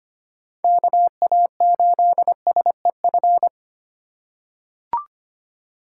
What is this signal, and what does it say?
Morse code, heard as a keyed tone, about 25 wpm, 700 Hz. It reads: KA8SEF
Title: USA Call Signs - 6 Characters - 2x Character Spacing - 25wpm